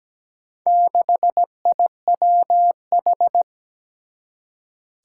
Morse code, heard as a keyed tone, about 17 wpm, 700 Hz. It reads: 6IWH
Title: Four Challenging Letter-Number Combinations 17wpm